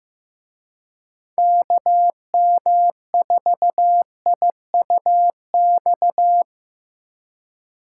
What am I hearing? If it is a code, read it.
Morse code, heard as a keyed tone, about 18 wpm, 700 Hz. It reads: KM4IUX
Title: Speed Racing - USA 2x3 Call Signs 15wpm